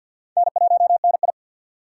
Morse code, heard as a keyed tone, about 48 wpm, 700 Hz. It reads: N1NI
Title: USA Call Signs - 4 Characters - 2x Character Spacing - 50wpm